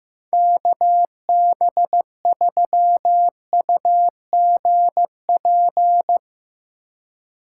Morse code, heard as a keyed tone, about 15 wpm, 700 Hz. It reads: KB3UGP
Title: USA Call Signs - 4x Character Spacing - 15wpm